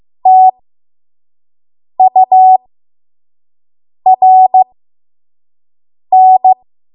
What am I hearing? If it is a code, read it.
Morse code, heard as a keyed tone, about 15 wpm, spaced as for 5 wpm, 750 Hz. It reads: TURN
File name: Morse / 05WPM